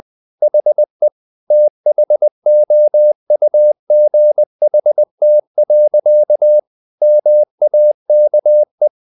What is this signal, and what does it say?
Morse code, heard as a keyed tone, about 20 wpm, 600 Hz. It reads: HE THOUGHT. MAKE
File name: the old man and the sea part 2 20wpm